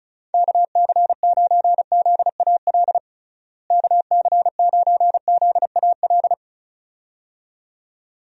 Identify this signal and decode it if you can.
Morse code, heard as a keyed tone, about 34 wpm, 700 Hz. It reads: KC9ZAL KC9ZAL
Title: USA Call Signs - 6 Characters - Repeated 2x - 35wpm